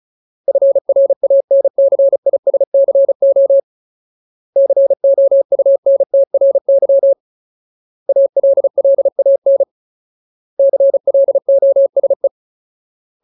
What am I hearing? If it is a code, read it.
Morse code, heard as a keyed tone, about 34 wpm, 550 Hz. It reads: FRANCISCO COUNTRY ALLAN CLOSE
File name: Ultimate Word List / 35WPM Ultimate